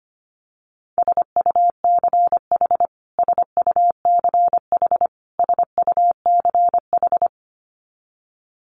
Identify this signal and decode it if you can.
Morse code, heard as a keyed tone, about 25 wpm, 700 Hz. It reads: SUCH SUCH SUCH
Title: Top 500 Words Repeated 25wpm